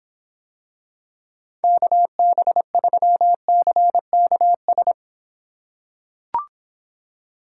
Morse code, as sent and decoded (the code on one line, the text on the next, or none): -.- -... ...-- -.-. -.- ...
KB3CKS